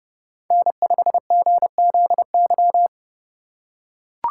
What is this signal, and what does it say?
Morse code, heard as a keyed tone, about 29 wpm, 700 Hz. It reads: N5GZY